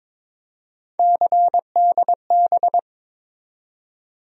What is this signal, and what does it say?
Morse code, heard as a keyed tone, about 22 wpm, 700 Hz. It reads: CDB
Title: USA Call Sign Postfixes - 3 Letters 22wpm